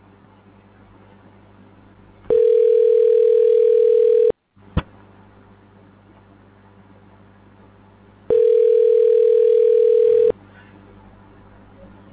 Ringback tone (North America), 440 + 480 Hz.
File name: recordings / RE35daa1874be18b371dda3046a0a40c6f